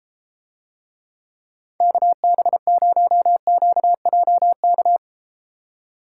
Morse code, heard as a keyed tone, about 34 wpm, 700 Hz. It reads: KB0QJK